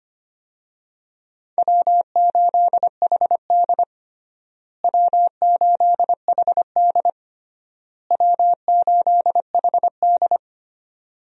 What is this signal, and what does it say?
Morse code, heard as a keyed tone, about 25 wpm, 700 Hz. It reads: W8HD W8HD W8HD
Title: USA Call Signs - 4 Characters - Repeated 3x - 25wpm